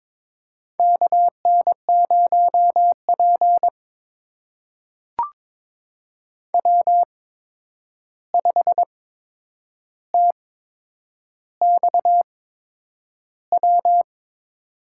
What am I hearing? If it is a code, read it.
Morse code, heard as a keyed tone, about 22 wpm, spaced as for 9 wpm, 700 Hz. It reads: KN0P W5TXW
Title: USA Call Signs - 8x Character Spacing - 22wpm